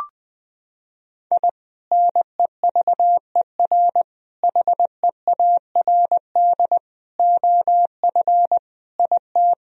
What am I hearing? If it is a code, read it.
Morse code, heard as a keyed tone, about 20 wpm, 700 Hz. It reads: I NEVER HEARD OF IT
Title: Sentences from Top 1000 Words - Review Part 1 - 20wpm